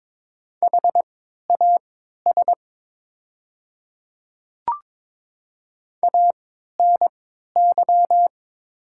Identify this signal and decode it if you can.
Morse code, heard as a keyed tone, about 22 wpm, spaced as for 13 wpm, 690 Hz. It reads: HAS ANY
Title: 3x Character Spacing - Top 500 Words - 22wpm